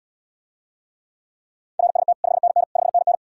Morse code, heard as a keyed tone, about 79 wpm, 700 Hz. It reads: RST 5NN 5NN